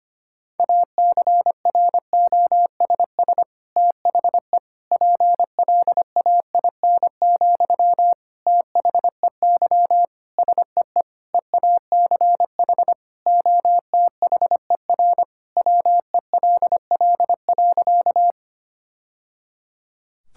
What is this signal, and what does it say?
Morse code, heard as a keyed tone, about 25 wpm, 700 Hz. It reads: ACROSS THE PLAIN, THEY SEE EACH OTHER WELL.
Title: Sentences from Top 1000 Words - Review Part 5 - 25wpm